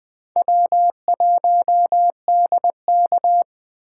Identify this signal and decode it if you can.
Morse code, heard as a keyed tone, about 20 wpm, 700 Hz. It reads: W1DK
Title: USA Call Signs - 4 Characters - 3x Character Spacing - 20wpm